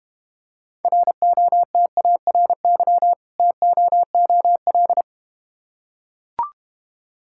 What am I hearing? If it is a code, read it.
Morse code, heard as a keyed tone, about 32 wpm, 700 Hz. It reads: ROTARY TOOL